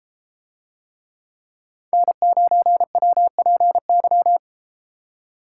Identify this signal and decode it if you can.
Morse code, heard as a keyed tone, about 32 wpm, 700 Hz. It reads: N9WPY